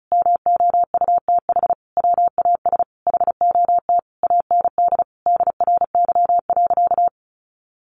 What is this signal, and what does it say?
Morse code, heard as a keyed tone, about 34 wpm, 700 Hz. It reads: MOUTH WAS HOT AND DRY.
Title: Sentences from Top 1500 Words 35wpm